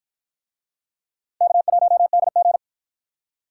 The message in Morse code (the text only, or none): K1DG